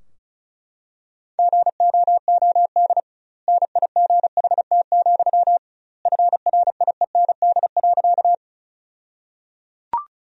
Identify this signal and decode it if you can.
Morse code, heard as a keyed tone, about 36 wpm, 700 Hz. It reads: GOOD NIGHT, FRIEND.
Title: 2x Word-Spacing - Sentences from Top 300 Words - 35wpm